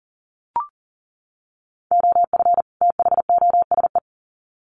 Morse code, heard as a keyed tone, about 40 wpm, 700 Hz. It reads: OF THOSE
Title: Sets of 2 Words 40wpm